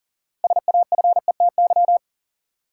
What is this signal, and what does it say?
Morse code, heard as a keyed tone, about 39 wpm, 700 Hz. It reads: SAFETY
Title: Six Letter Words - 40wpm